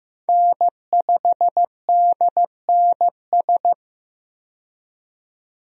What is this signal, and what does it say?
Morse code, heard as a keyed tone, about 15 wpm, 700 Hz. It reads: N5DNS
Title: USA Call Signs - 5 Characters - 15wpm